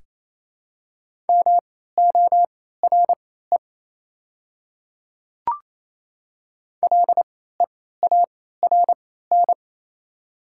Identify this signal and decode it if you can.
Morse code, heard as a keyed tone, about 27 wpm, spaced as for 16 wpm, 700 Hz. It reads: MORE LEARN
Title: 3x Character Spacing - Top 300 Words - 28wpm